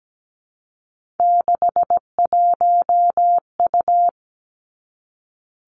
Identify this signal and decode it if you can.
Morse code, heard as a keyed tone, about 17 wpm, 700 Hz. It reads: 61U
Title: Three Challenging Letter-Number Combinations 17wpm